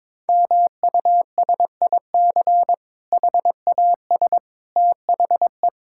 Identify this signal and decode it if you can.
Morse code, heard as a keyed tone, about 21 wpm, 700 Hz. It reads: MUSIC HAS THE
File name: Difficult Sentences with Hints - 22wpm